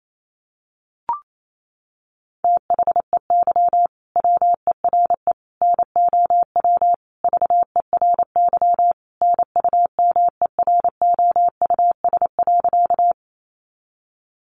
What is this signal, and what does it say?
Morse code, heard as a keyed tone, about 28 wpm, 700 Hz. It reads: THEY WERE NOW VERY NUMEROUS.
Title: Sentences from Top 1400 Words 28wpm